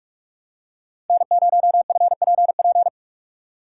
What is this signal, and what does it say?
Morse code, heard as a keyed tone, about 44 wpm, 700 Hz. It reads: N0FPP
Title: USA Call Signs - 6x Character Spacing - 45wpm